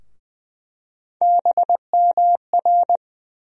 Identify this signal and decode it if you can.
Morse code, heard as a keyed tone, about 20 wpm, 700 Hz. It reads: BMR
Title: USA Call Sign Postfixes - 3 Letters 20wpm